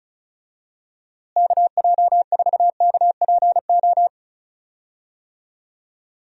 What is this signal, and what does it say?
Morse code, heard as a keyed tone, about 34 wpm, 700 Hz. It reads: KJ4KPO
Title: USA Call Signs - 6 Characters - Repeated 2x - 35wpm